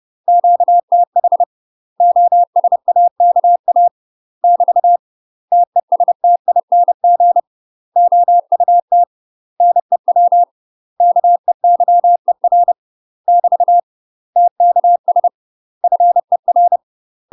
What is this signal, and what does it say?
Morse code, heard as a keyed tone, about 30 wpm, 700 Hz. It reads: QTH OSAKA <BT> TESTING OUT NEW KEYER <BT> TKS FER
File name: UK MASTER QSOs TRAINING / 30WPM / UK MASTER QSOs 0015